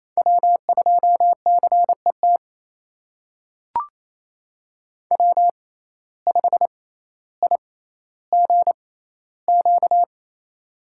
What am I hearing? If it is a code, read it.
Morse code, heard as a keyed tone, about 28 wpm, spaced as for 14 wpm, 700 Hz. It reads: W2CET W5IGQ
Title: USA Call Signs - 5 Characters - 6x Character Spacing - 28wpm